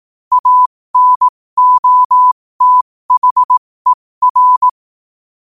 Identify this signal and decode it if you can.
Morse code, heard as a keyed tone, about 18 wpm, spaced as for 15 wpm, 1000 Hz. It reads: ANOTHER
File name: qotdfiles / 15wpm